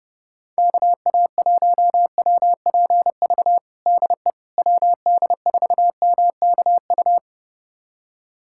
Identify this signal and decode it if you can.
Morse code, heard as a keyed tone, about 30 wpm, 700 Hz. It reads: KA1WPV DE WD4MKU
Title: USA Callsign DE Callsign - 6 Characters - 30wpm